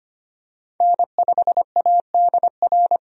Morse code, heard as a keyed tone, about 25 wpm, 700 Hz. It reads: N5ADR
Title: USA Call Signs - 5 Characters - 25wpm